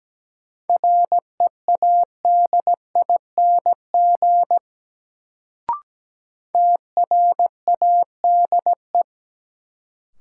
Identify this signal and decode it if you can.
Morse code, heard as a keyed tone, about 17 wpm, 700 Hz. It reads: READING TRADE
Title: Sentences from Top 600 Words - New Word Drill - 17wpm